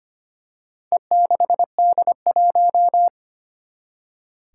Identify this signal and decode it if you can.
Morse code, heard as a keyed tone, about 25 wpm, 700 Hz. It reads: E6D1
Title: Four Challenging Letter-Number Combinations 25wpm